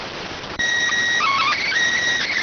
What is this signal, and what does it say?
SSTV (slow-scan television), PD 290.